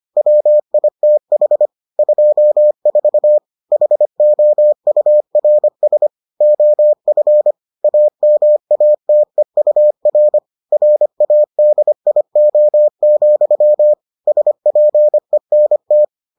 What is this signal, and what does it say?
Morse code, heard as a keyed tone, about 25 wpm, 600 Hz. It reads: WITH 24 HOURS OF AMATEUR RADIO, SPENT